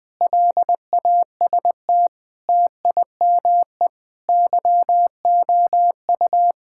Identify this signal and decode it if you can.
Morse code, heard as a keyed tone, about 20 wpm, 700 Hz. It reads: LAST TIME YOU
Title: Sets of 6 Words - 20wpm